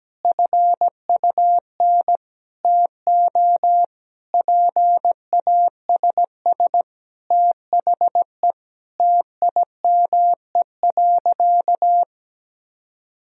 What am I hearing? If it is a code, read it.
Morse code, heard as a keyed tone, about 17 wpm, 700 Hz. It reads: FUN TO PASS THE TIME.
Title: Sentences from Top 1000 Words - Review Part 4 - 17wpm